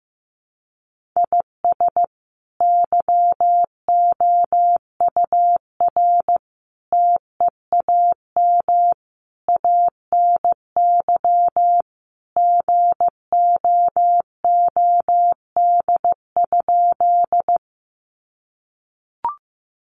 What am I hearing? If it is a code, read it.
Morse code, heard as a keyed tone, about 14 wpm, 700 Hz. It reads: IS YOUR TEAM ANY GOOD?
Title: Sentences from Top 1500 Words 15wpm